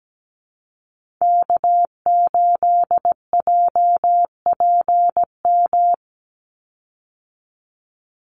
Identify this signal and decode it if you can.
Morse code, heard as a keyed tone, about 17 wpm, 700 Hz. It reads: K8JPM
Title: Call Signs Repeated 2x - 17wpm